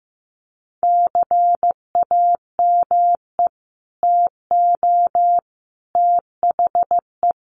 Morse code, heard as a keyed tone, about 15 wpm, 700 Hz. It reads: CAME TO THE
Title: Sets of 5 Words Encore2 15wpm